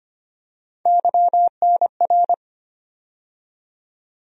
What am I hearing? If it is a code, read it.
Morse code, heard as a keyed tone, about 25 wpm, 700 Hz. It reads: YNR
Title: USA Call Sign Postfixes - 3 Letters 25wpm